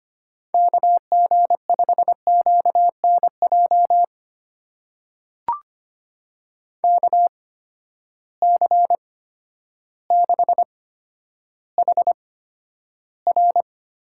Morse code, heard as a keyed tone, about 25 wpm, spaced as for 11 wpm, 700 Hz. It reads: KG5QNJ KC6HR
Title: USA Call Signs - 6 Characters - 8x Character Spacing - 25wpm